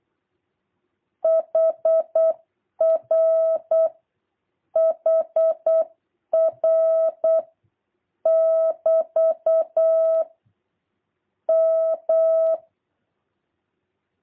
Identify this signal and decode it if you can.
Morse code, heard as a keyed tone, about 8 wpm, 650 Hz. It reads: HR HR <BT> M
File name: jh8cbh / 2023cw